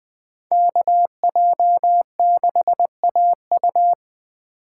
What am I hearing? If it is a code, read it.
Morse code, heard as a keyed tone, about 20 wpm, 700 Hz. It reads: KJ6AU